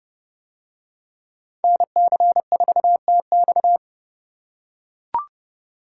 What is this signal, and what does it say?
Morse code, heard as a keyed tone, about 29 wpm, 700 Hz. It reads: NC4TX